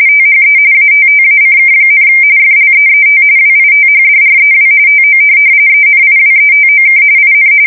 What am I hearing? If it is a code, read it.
RTTY, about 45.45 baud. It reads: THE ATTACK, FLEET WITHDREW TO ENIWETOK IN
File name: fsk170 / USS-Massachusetts